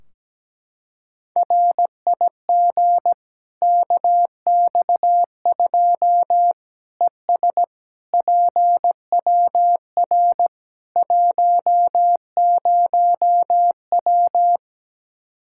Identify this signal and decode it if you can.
Morse code, heard as a keyed tone, about 17 wpm, 700 Hz. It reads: RIG KX2 ES PWR 10W
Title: K1USN SST QSO Practice 17wpm